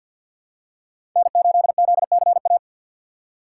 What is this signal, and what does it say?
Morse code, heard as a keyed tone, about 52 wpm, 700 Hz. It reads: N8ZCA
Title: USA Call Signs - 5 Characters - 2x Character Spacing - 50wpm